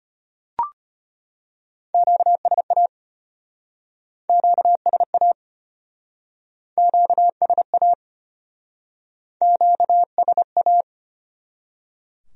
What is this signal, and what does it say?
Morse code, heard as a keyed tone, about 34 wpm, 700 Hz. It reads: QSA QSA QSA QSA